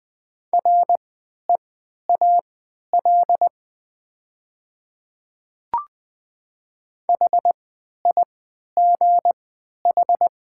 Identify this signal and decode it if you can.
Morse code, heard as a keyed tone, about 20 wpm, spaced as for 11 wpm, 700 Hz. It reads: REAL HIGH
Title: 3x Character Spacing - Top 500 Words - 20wpm